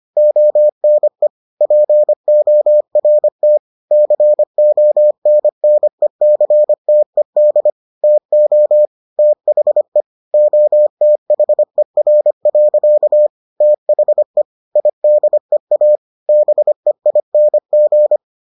Morse code, heard as a keyed tone, about 25 wpm, 600 Hz. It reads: ONE PORT CONNECTED TO THE OTHER. THE IDEA BEING